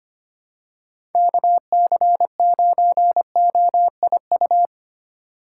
Morse code, heard as a keyed tone, about 25 wpm, 700 Hz. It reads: KC9OIU
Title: USA Call Signs - 6 Characters - 2x Character Spacing - 25wpm